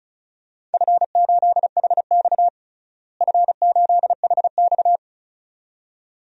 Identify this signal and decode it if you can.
Morse code, heard as a keyed tone, about 36 wpm, 700 Hz. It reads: F8HX F8HX
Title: DXCC-100 Call Signs - 4 Characters - Repeated 2x - 35wpm